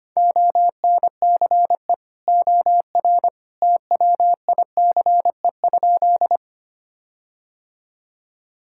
Morse code, heard as a keyed tone, about 25 wpm, 700 Hz. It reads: ONCE OR TWICE?
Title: Sentences from Top 1600 Words 25wpm